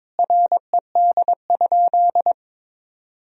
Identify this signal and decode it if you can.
Morse code, heard as a keyed tone, about 22 wpm, 700 Hz. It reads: RED?
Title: Sentences from Top 1000 Words - Review Part 2 - 22wpm